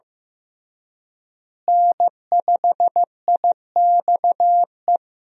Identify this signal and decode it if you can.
Morse code, heard as a keyed tone, about 15 wpm, 700 Hz. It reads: N5IXE
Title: USA Call Signs - 5 Characters - Repeated 3x - 15wpm